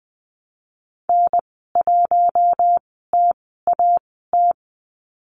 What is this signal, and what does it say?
Morse code, heard as a keyed tone, about 20 wpm, spaced as for 15 wpm, 700 Hz. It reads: N1TAT
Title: 062 Lesson 11g - Letter h - Call Signs 20wpm